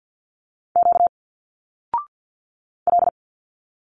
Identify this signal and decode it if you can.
Morse code, heard as a keyed tone, about 49 wpm, 700 Hz. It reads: Q L
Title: Single Letters 50wpm